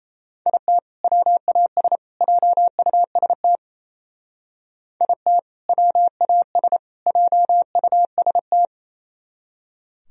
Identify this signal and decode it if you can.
Morse code, heard as a keyed tone, about 34 wpm, 700 Hz. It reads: IT WAS JUST IT WAS JUST
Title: Speed-Racing - Sets of 3 Words 28wpm